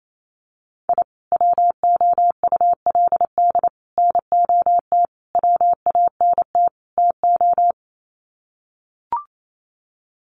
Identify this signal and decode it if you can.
Morse code, heard as a keyed tone, about 28 wpm, 700 Hz. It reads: I WOULD NOT WANT TO
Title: Sets of 5 Words 28wpm